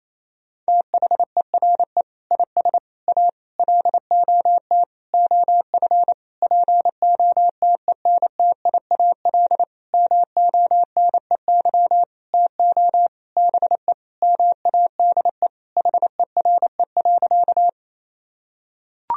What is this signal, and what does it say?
Morse code, heard as a keyed tone, about 28 wpm, 700 Hz. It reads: THERE IS A LOT OF POTENTIAL MONEY TO BE MADE HERE.
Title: Sentences from Top 1000 Words - Review Part 4 - 28wpm